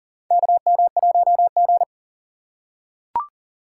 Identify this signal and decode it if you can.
Morse code, heard as a keyed tone, about 40 wpm, 700 Hz. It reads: KM1G
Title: USA Call Signs - 4 Characters - 6x Character Spacing - 40wpm